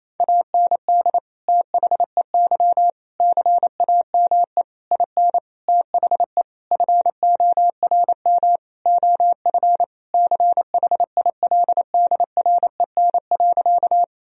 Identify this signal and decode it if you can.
Morse code, heard as a keyed tone, about 28 wpm, 700 Hz. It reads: AND THEY CAME IN THE FORM OF CHILDREN.
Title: Sentences from Top 1000 Words - Review Part 1 - 28wpm